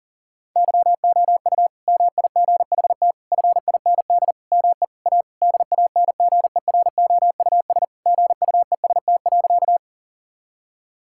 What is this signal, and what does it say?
Morse code, heard as a keyed tone, about 40 wpm, 700 Hz. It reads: YOU MIGHT FIND ME A DANGEROUS GUEST.
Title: Sentences from Top 1600 Words 40wpm